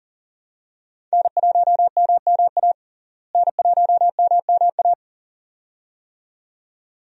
Morse code, heard as a keyed tone, about 40 wpm, 700 Hz. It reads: N1MMA N1MMA
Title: USA Call Signs - 5 Characters - Repeated 2x - 40wpm